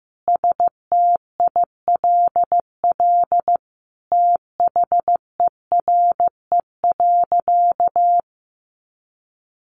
Morse code, heard as a keyed tone, about 14 wpm, 700 Hz. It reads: STILL THERE.
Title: Sentences from Top 1500 Words 15wpm